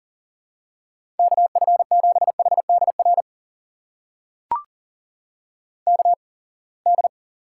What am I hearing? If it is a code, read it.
Morse code, heard as a keyed tone, about 40 wpm, spaced as for 28 wpm, 700 Hz. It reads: KF7HDR KD
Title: USA Call Signs - 6 Characters - 8x Character Spacing - 40wpm